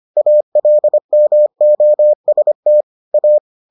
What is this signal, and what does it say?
Morse code, heard as a keyed tone, about 25 wpm, 600 Hz. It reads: ALMOST A